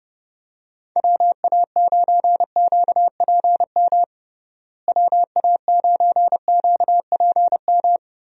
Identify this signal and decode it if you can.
Morse code, heard as a keyed tone, about 30 wpm, 700 Hz. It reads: WA9QPM WA9QPM
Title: USA Call Signs - 6 Characters - Repeated 3x - 30wpm